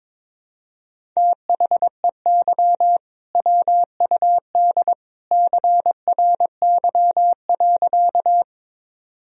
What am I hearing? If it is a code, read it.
Morse code, heard as a keyed tone, about 22 wpm, 700 Hz. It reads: THEY WUD CRY.
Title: Sentences from Top 500 Words and Abbreviations 22wpm